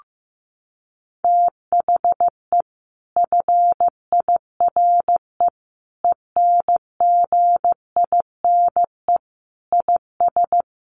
Morse code, heard as a keyed tone, about 14 wpm, 700 Hz. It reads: THE FIRE ENGINE IS
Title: Sentences from Top 1500 Words 15wpm